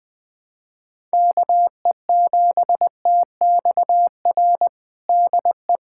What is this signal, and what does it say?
Morse code, heard as a keyed tone, about 20 wpm, 700 Hz. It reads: KE7TXR DE
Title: USA Callsign DE Callsign - 6 Characters - 20wpm